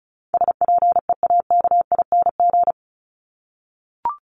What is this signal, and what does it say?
Morse code, heard as a keyed tone, about 34 wpm, 700 Hz. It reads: SPEAKING
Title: ING Suffix - Top 100 Words with Base Words 35wpm